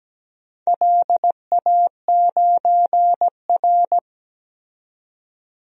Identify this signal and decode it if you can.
Morse code, heard as a keyed tone, about 17 wpm, 700 Hz. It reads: LA9R
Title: DXCC-100 Call Signs - 4 Characters - 17wpm